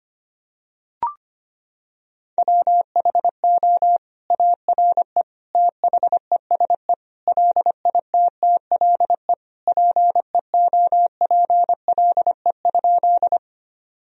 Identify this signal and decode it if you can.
Morse code, heard as a keyed tone, about 25 wpm, 700 Hz. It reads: WHO ARE THESE LITTLE PEOPLE?
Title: Sentences from Top 1000 Words - Review Part 1 - 25wpm